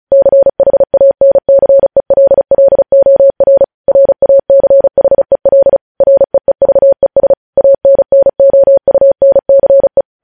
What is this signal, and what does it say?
Morse code, heard as a keyed tone, about 36 wpm, 550 Hz. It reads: CHANCELLOR RACHEL REEVES ANNOUNCE
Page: news headlines in morse code at 35 wpm